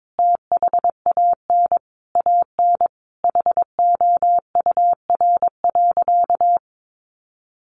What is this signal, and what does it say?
Morse code, heard as a keyed tone, about 22 wpm, 700 Hz. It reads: THAN AN HOUR.
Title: Sentences from Top 500 Words 22wpm